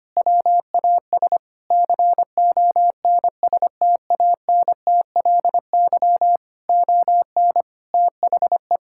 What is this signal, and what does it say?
Morse code, heard as a keyed tone, about 25 wpm, 700 Hz. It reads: WAS CONSTANTLY ON THE
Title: Sentences from Top 1600 Words 25wpm